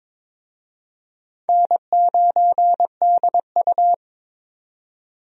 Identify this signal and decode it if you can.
Morse code, heard as a keyed tone, about 23 wpm, 700 Hz. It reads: N9DU
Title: USA Call Signs - 4 Characters - 22wpm